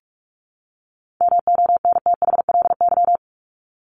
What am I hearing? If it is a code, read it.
Morse code, heard as a keyed tone, about 44 wpm, 700 Hz. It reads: MONTHLY
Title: LY Suffix - Top 100 Words with Base Words 45wpm